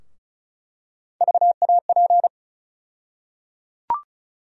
Morse code, heard as a keyed tone, about 36 wpm, 700 Hz. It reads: VAP
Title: USA Call Sign Postfixes - 3 Letters 35wpm